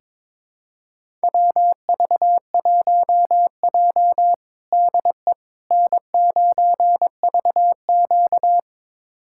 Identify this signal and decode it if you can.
Morse code, heard as a keyed tone, about 22 wpm, 700 Hz. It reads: WV1J DE N9VQ
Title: USA Callsign DE Callsign - 4 Characters - 22wpm